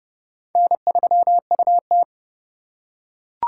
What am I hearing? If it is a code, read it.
Morse code, heard as a keyed tone, about 26 wpm, 700 Hz. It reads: N3UT